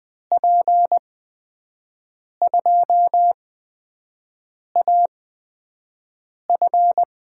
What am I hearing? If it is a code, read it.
Morse code, heard as a keyed tone, about 20 wpm, spaced as for 5 wpm, 700 Hz. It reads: P2AF
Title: USA Call Signs - 5 Characters - 8x Character Spacing - 20wpm